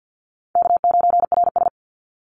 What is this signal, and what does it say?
Morse code, heard as a keyed tone, about 50 wpm, 700 Hz. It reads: K9RS
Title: Easy Call Signs Repeated 3x - 50wpm